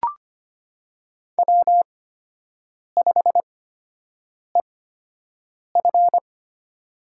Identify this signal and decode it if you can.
Morse code, heard as a keyed tone, about 25 wpm, spaced as for 7 wpm, 700 Hz. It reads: W5EF
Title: USA Call Signs - 4 Characters - 8x Character Spacing - 25wpm